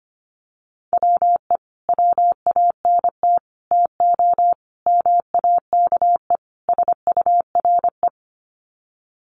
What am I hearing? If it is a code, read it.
Morse code, heard as a keyed tone, about 25 wpm, 700 Hz. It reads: WE WANT TO MAKE SURE
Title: Sets of 5 Words Encore2 25wpm